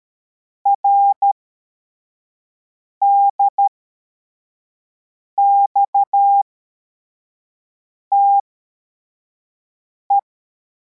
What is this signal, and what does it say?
Morse code, heard as a keyed tone, about 13 wpm, spaced as for 4 wpm, 800 Hz. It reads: RDXTE